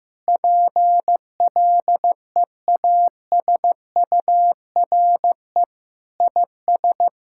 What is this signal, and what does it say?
Morse code, heard as a keyed tone, about 16 wpm, 700 Hz. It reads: PLEASURE IS
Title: Sentences from Top 1600 Words 15wpm